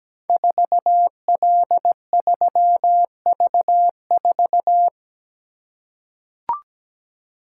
Morse code, heard as a keyed tone, about 17 wpm, 690 Hz. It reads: 4L3V4
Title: Code Groups - 5 Letter:Numbers - 17wpm